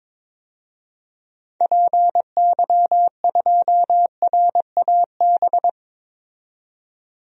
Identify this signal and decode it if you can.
Morse code, heard as a keyed tone, about 21 wpm, 700 Hz. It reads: PY2RAB
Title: DXCC-100 Call Signs - 6 Characters - 22wpm